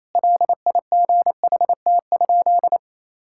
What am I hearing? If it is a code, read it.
Morse code, heard as a keyed tone, about 27 wpm, 700 Hz. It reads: LIGHT?
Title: Sentences from Top 1000 Words - Review Part 5 - 28wpm